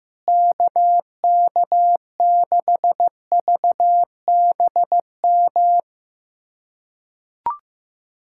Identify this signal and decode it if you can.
Morse code, heard as a keyed tone, about 15 wpm, 700 Hz. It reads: KK6VBM
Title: USA Callsign DE Callsign - 6 Characters - 15wpm